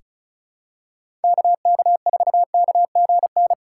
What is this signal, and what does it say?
Morse code, heard as a keyed tone, about 36 wpm, 700 Hz. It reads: KK4KGN
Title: USA Call Signs - 6 Characters - 2x Character Spacing - 35wpm